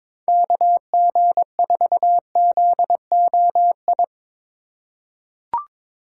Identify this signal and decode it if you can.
Morse code, heard as a keyed tone, about 22 wpm, 700 Hz. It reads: KG4ZOI
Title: USA Call Signs - 6 Characters - 2x Character Spacing - 22wpm